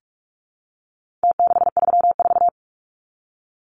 Morse code, heard as a keyed tone, about 44 wpm, 700 Hz. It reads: T634
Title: Numbers - Contest Style - 4-Digit 45wpm